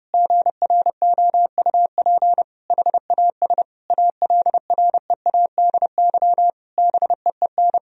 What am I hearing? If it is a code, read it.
Morse code, heard as a keyed tone, about 30 wpm, 700 Hz. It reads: GROUP HAS ALREADY BEEN
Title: Sentences from Top 1000 Words - Review Part 1 - 30wpm